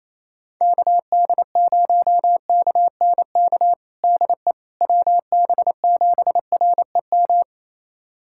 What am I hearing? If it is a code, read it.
Morse code, heard as a keyed tone, about 27 wpm, 700 Hz. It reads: KD0KNK DE WB7REM
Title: USA Callsign DE Callsign - 6 Characters - 28wpm